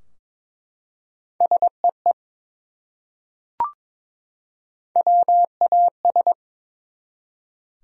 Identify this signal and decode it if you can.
Morse code, heard as a keyed tone, about 21 wpm, 700 Hz. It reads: SEE WAS
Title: Top 100 Words 22wpm